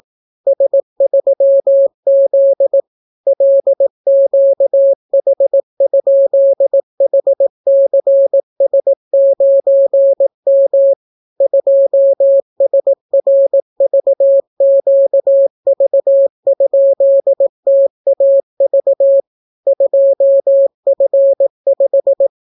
Morse code, heard as a keyed tone, about 18 wpm, 550 Hz. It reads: S3Z LQH?HCS9M 2SRVQV?TAV 2F5